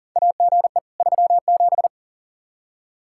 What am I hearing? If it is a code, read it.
Morse code, heard as a keyed tone, about 40 wpm, 700 Hz. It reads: AGE 37
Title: K1USN SST QSO Practice 40wpm